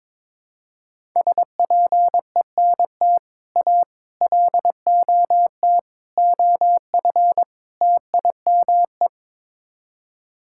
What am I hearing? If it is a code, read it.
Morse code, heard as a keyed tone, about 22 wpm, 700 Hz. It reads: SPENT A LOT OF TIME
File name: practice / 22wpm / Build A Phrase - 5-Word Phrases - 22wpm